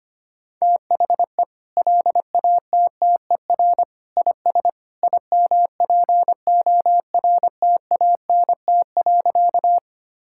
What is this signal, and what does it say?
Morse code, heard as a keyed tone, about 25 wpm, 700 Hz. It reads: THE LATTER IS IMPORTANT.
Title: Sentences from Top 1600 Words 25wpm